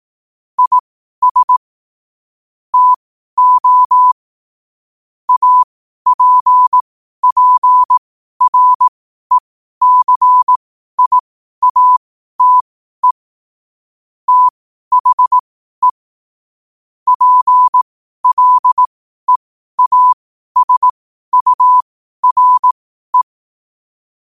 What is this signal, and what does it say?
Morse code, heard as a keyed tone, about 18 wpm, spaced as for 12 wpm, 1000 Hz. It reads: IS TO APPRECIATE THE PLEASURE
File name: qotdfiles / 12wpm